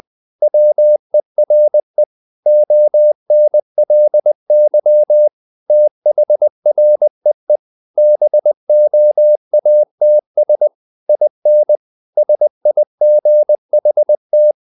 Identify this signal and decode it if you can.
Morse code, heard as a keyed tone, about 20 wpm, 600 Hz. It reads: WERE ONLY THREE BOATS IN SIGHT
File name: the old man and the sea / the old man and the sea part 1 20wpm